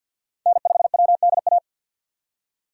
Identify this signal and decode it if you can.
Morse code, heard as a keyed tone, about 52 wpm, 700 Hz. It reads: N5WDA
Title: USA Call Signs - 5 Characters - 2x Character Spacing - 50wpm